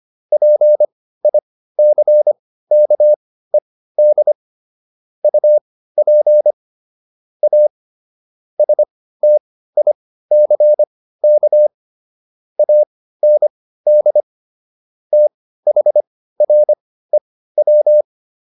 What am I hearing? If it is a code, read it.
Morse code, heard as a keyed tone, about 25 wpm, spaced as for 15 wpm, 600 Hz. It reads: PICKED UP A STICK AND THREW